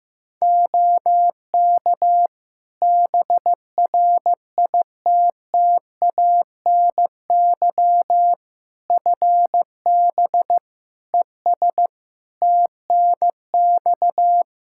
Morse code, heard as a keyed tone, about 15 wpm, 700 Hz. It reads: OK BRITTANY FB ES TNX
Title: K1USN SST QSO Practice 15wpm